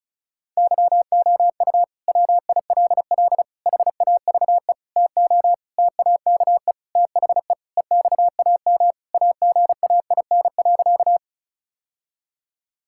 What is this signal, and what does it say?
Morse code, heard as a keyed tone, about 36 wpm, 700 Hz. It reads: YOU WILL HAVE TO TAKE THE EXAM AGAIN.
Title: Sentences from Top 1000 Words - Review Part 5 - 35wpm